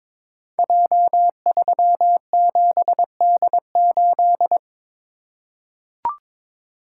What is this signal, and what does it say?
Morse code, heard as a keyed tone, about 22 wpm, 700 Hz. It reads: J37D8
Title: Code Groups - 5 Letter:Numbers - 22wpm